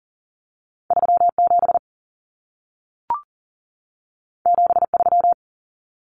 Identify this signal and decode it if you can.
Morse code, heard as a keyed tone, about 40 wpm, 700 Hz. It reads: 37 73
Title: Common Cut Numbers - 2-Digit 40wpm